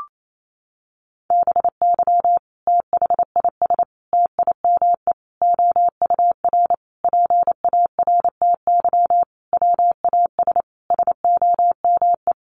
Morse code, heard as a keyed tone, about 28 wpm, 700 Hz. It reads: BY THIS TIME OUR PARTY WAS SOME
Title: Sentences from Top 1400 Words 28wpm